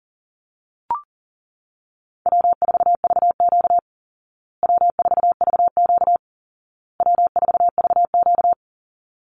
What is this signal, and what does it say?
Morse code, heard as a keyed tone, about 37 wpm, 700 Hz. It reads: W4VQ W4VQ W4VQ
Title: Call Signs Repeated 2x - 40wpm